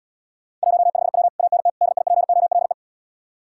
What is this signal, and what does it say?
Morse code, heard as a keyed tone, about 80 wpm, 700 Hz. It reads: 15W ANT DIPOLE